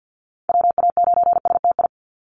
Morse code, heard as a keyed tone, about 50 wpm, 700 Hz. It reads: WA9STI
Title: NA SOTA Activator Callsigns 50wpm